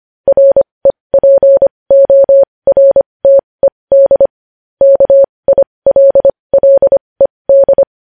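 Morse code, heard as a keyed tone, about 25 wpm, spaced as for 20 wpm, 550 Hz. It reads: REPORTED KILLED
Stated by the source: Morse code news headlines sent at 20 wpm.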